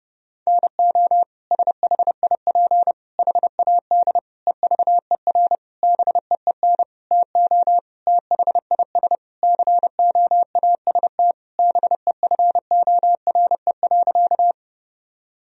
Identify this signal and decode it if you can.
Morse code, heard as a keyed tone, about 30 wpm, 700 Hz. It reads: NO SHIP HAD EVER BEEN TO THIS COAST BEFORE.
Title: Sentences from Top 1000 Words - Review Part 4 - 30wpm